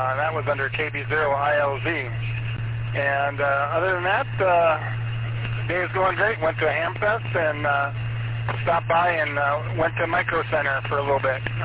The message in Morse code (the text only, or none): none